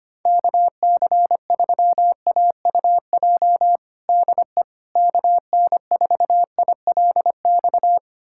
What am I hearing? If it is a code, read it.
Morse code, heard as a keyed tone, about 25 wpm, 700 Hz. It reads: KC3AUJ DE KN4ILX
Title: USA Callsign DE Callsign - 6 Characters - 25wpm